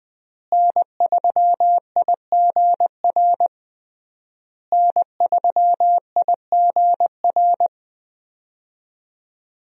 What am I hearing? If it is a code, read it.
Morse code, heard as a keyed tone, about 21 wpm, 700 Hz. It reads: N3IGR N3IGR
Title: USA Call Signs - 5 Characters - Repeated 3x - 20wpm